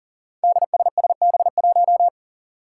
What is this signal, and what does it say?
Morse code, heard as a keyed tone, about 40 wpm, 700 Hz. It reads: DSSB1
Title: Five Challenging Letter-Number Combinations 40wpm